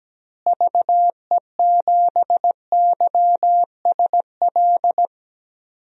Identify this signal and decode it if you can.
Morse code, heard as a keyed tone, about 17 wpm, 700 Hz. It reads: VE7YSL
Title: DXCC-100 Call Signs - 6 Characters - Repeated 2x - 17wpm